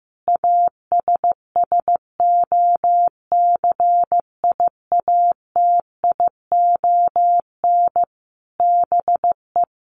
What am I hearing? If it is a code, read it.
Morse code, heard as a keyed tone, about 14 wpm, 700 Hz. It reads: ASSOCIATION BE
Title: Sentences from Top 1300 Words 15wpm